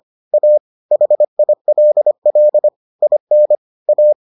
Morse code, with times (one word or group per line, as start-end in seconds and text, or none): 0.3-0.6 A
0.9-2.7 HILL
3.0-3.5 IN
3.9-4.1 A